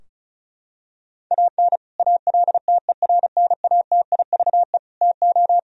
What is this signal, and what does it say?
Morse code, heard as a keyed tone, about 34 wpm, 700 Hz. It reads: AN ALTERNATIVE TO
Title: Sentences from Top 1000 Words - Review Part 4 - 35wpm